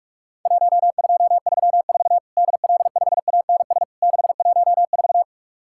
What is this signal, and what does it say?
Morse code, heard as a keyed tone, about 44 wpm, 700 Hz. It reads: 1234 DL5ANS 614
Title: Worked All Europe DX Contest - QTC Exchange - 45wpm